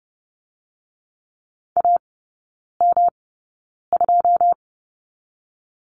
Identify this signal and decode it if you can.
Morse code, heard as a keyed tone, about 30 wpm, 700 Hz. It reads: A M 2
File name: Lesson 19e - Letter w - Three Characters 30wpm